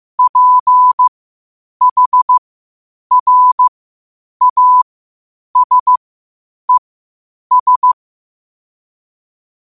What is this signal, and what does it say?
Morse code, heard as a keyed tone, about 15 wpm, spaced as for 8 wpm, 1000 Hz. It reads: PHRASES
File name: qotdfiles / 8wpm